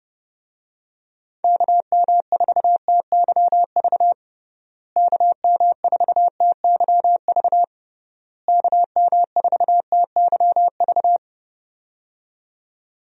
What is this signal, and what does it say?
Morse code, heard as a keyed tone, about 32 wpm, 700 Hz. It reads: KM4TYV KM4TYV KM4TYV
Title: USA Call Signs - 6 Characters - Repeated 3x - 30wpm